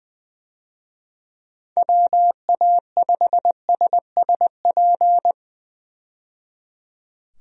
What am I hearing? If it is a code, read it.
Morse code, heard as a keyed tone, about 20 wpm, 700 Hz. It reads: WA5SSP
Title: USA Call Signs - 6 Characters - 20wpm